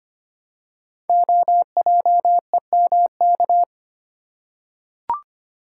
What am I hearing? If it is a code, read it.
Morse code, heard as a keyed tone, about 25 wpm, 700 Hz. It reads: OJEMK
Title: Code Groups - 5 Letters - 25wpm